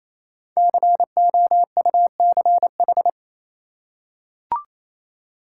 Morse code, heard as a keyed tone, about 27 wpm, 700 Hz. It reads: COUCH
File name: Easy Words - Speed-Racing with Hints - 22wpm